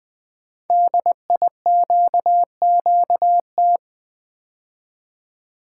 Morse code, heard as a keyed tone, about 20 wpm, 700 Hz. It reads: DIQQT
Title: Code Groups - 5 Letters - 20wpm